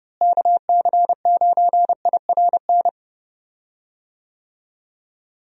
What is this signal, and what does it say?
Morse code, heard as a keyed tone, about 31 wpm, 700 Hz. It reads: KC9IRN
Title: USA Call Signs - 6 Characters - Repeated 3x - 30wpm